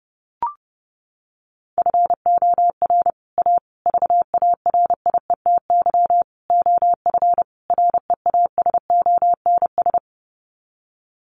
Morse code, heard as a keyed tone, about 30 wpm, 700 Hz. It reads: FOR A VARIETY OF REASONS
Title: Sets of 5 Words Encore 30wpm